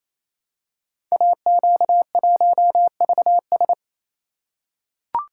Morse code, heard as a keyed tone, about 27 wpm, 700 Hz. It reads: AQ1VS
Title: Code Groups - 5 Letter:Numbers - 28wpm